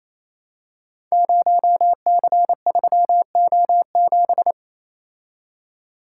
Morse code, heard as a keyed tone, about 29 wpm, 700 Hz. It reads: 0C3O7
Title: Code Groups - 5 Letter:Numbers - 28wpm